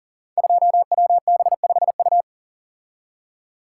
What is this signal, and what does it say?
Morse code, heard as a keyed tone, about 38 wpm, 700 Hz. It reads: 2WB5U
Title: Code Groups - 5 Letter:Numbers - 40wpm